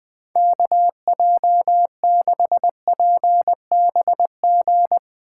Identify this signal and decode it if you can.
Morse code, heard as a keyed tone, about 21 wpm, 700 Hz. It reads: KJ6PBG
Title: USA Call Signs - 6 Characters - 2x Character Spacing - 20wpm